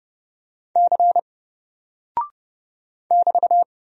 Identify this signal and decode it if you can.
Morse code, heard as a keyed tone, about 30 wpm, 700 Hz. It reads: C <BT>
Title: Single Character (Letter:Number:Punctuation:Prosign) Alt 30wpm